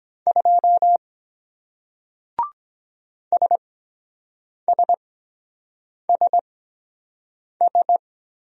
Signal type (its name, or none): Morse code